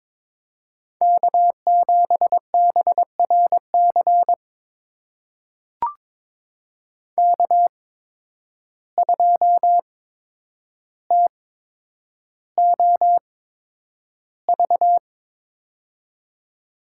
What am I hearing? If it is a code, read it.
Morse code, heard as a keyed tone, about 22 wpm, spaced as for 9 wpm, 700 Hz. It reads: K7BRC K2TOV
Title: USA Call Signs - 5 Characters - 8x Character Spacing - 22wpm